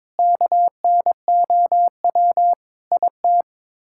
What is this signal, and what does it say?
Morse code, heard as a keyed tone, about 22 wpm, 700 Hz. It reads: KNOW IT
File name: Sets of 9 Words - 22wpm